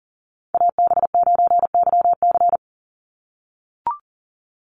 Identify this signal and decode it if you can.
Morse code, heard as a keyed tone, about 40 wpm, 700 Hz. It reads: AB9YC
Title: Call Signs - 40wpm